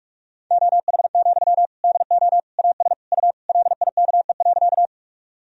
Morse code, heard as a keyed tone, about 46 wpm, 690 Hz. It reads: OH, DO AS U LIKE.
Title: Sentences from Top 500 Words and Abbreviations 45wpm